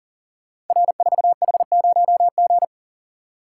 Morse code, heard as a keyed tone, about 39 wpm, 690 Hz. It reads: R4H0G
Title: Code Groups - 5 Letter:Numbers - 40wpm